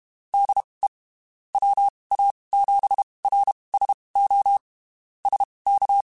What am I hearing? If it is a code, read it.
Morse code, heard as a keyed tone, about 31 wpm, spaced as for 23 wpm, 800 Hz. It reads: DE WA7RSO SK
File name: (spaced) / QSOs Spaced-mp3s / kc7sac @30 sp